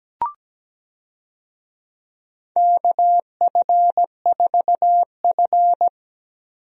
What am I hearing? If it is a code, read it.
Morse code, heard as a keyed tone, about 17 wpm, 700 Hz. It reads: KF4F
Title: USA Call Signs - 4 Characters - Repeated 3x - 17wpm